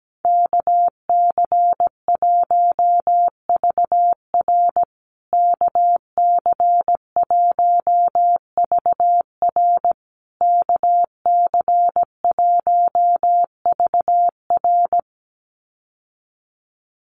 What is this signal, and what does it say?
Morse code, heard as a keyed tone, about 17 wpm, 700 Hz. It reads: KC1VR KC1VR KC1VR
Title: USA Call Signs Repeated 3x 17wpm